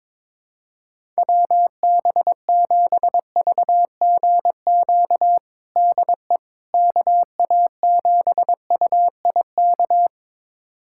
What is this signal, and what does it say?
Morse code, heard as a keyed tone, about 22 wpm, 700 Hz. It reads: WB7VGQ DE KA7UIK
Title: USA Callsign DE Callsign - 6 Characters - 22wpm